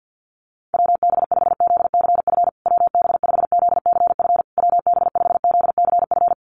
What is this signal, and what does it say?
Morse code, heard as a keyed tone, about 50 wpm, 700 Hz. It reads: WB5ZCF WB5ZCF WB5ZCF
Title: USA Call Signs Repeated 3x 50wpm